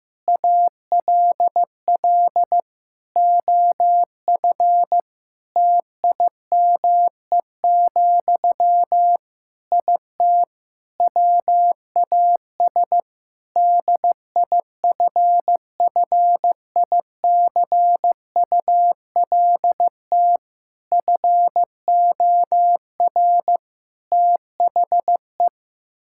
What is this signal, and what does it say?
Morse code, heard as a keyed tone, about 16 wpm, 700 Hz. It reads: ALL OF TIME, IT WAS DIFFICULT FOR THE
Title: Sentences from Top 1000 Words - Review Part 5 - 15wpm